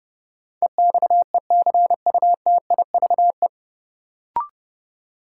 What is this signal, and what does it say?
Morse code, heard as a keyed tone, about 30 wpm, 700 Hz. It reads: EXECUTIVE
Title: Sets of 5 Words Encore - New Word Drill - 30wpm